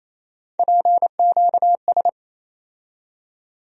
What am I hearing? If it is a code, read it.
Morse code, heard as a keyed tone, about 28 wpm, 700 Hz. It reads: PQS
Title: USA Call Sign Postfixes - 3 Letters 28wpm